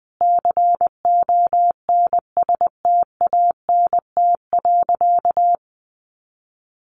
Morse code, heard as a keyed tone, about 20 wpm, 700 Hz. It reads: CONSTANT.
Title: Sentences from Top 1500 Words 20wpm